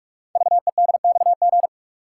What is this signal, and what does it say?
Morse code, heard as a keyed tone, about 44 wpm, 700 Hz. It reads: VEDXG
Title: Code Groups - 5 Letters - 45wpm